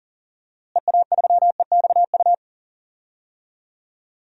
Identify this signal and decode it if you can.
Morse code, heard as a keyed tone, about 38 wpm, 700 Hz. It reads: EA3EXU
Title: DXCC-100 Call Signs - 6 Characters - Repeated 2x - 40wpm